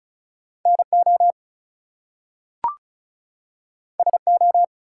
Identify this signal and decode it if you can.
Morse code, heard as a keyed tone, about 36 wpm, 700 Hz. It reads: NO SO
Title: Two Letter Words - 35wpm